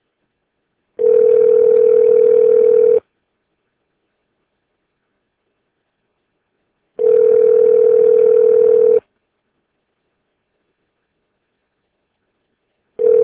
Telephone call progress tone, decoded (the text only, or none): ringback tone